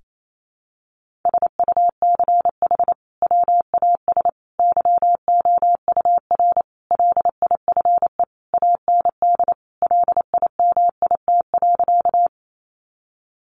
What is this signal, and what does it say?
Morse code, heard as a keyed tone, about 28 wpm, 700 Hz. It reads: SUCH WAS YOUR LIFE AND LIMIT.
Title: Sentences from Top 1500 Words 28wpm